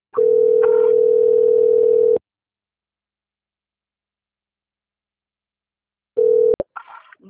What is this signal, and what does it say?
Ringback tone (North America), 440 + 480 Hz.
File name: out- - - - - .45106